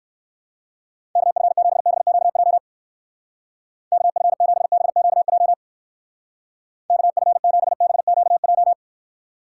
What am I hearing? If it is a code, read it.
Morse code, heard as a keyed tone, about 56 wpm, 700 Hz. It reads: KF7BQJ KF7BQJ KF7BQJ